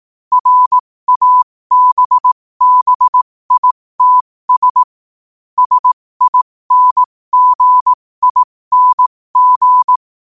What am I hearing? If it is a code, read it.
Morse code, heard as a keyed tone, about 18 wpm, spaced as for 15 wpm, 1000 Hz. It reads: RABBITS SINGING
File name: qotdfiles / 15wpm